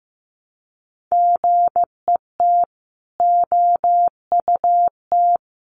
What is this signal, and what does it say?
Morse code, heard as a keyed tone, about 15 wpm, 700 Hz. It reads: GET OUT
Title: Sets of 5 Words Encore 15wpm